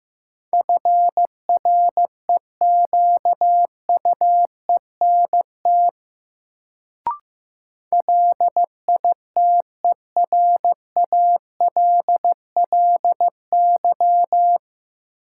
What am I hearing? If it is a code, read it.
Morse code, heard as a keyed tone, about 15 wpm, 700 Hz. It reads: FREQUENT LITERALLY
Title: Sentences from Top 1400 Words - New Word Drill Alt - 15wpm